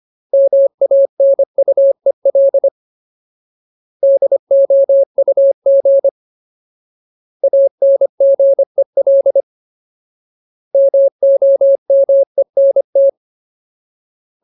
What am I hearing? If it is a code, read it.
Morse code, heard as a keyed tone, about 25 wpm, 550 Hz. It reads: MANUEL DOUG ANGEL MOMENT
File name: Ultimate Word List / 25WPM Ultimate